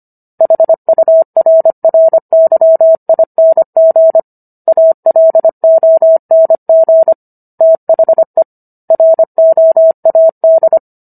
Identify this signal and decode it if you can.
Morse code, heard as a keyed tone, about 25 wpm, 650 Hz. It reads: HURRYING ALONG THE ROAD
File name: Book One / 16-Chapter 16 - 25 WPM